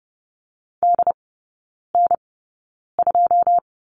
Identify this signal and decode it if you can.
Morse code, heard as a keyed tone, about 30 wpm, spaced as for 9 wpm, 690 Hz. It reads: DN2
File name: Lesson 17e - Letter u - Three Characters 30wpm